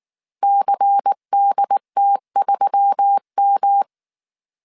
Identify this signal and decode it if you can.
Morse code, heard as a keyed tone, about 19 wpm, 800 Hz. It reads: CDT3M
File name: CW / DET / 1 6 1500C S A 18wpm 2255